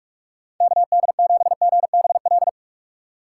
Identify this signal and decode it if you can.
Morse code, heard as a keyed tone, about 44 wpm, 700 Hz. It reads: KD7GBL
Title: USA Call Signs - 6 Characters - 6x Character Spacing - 45wpm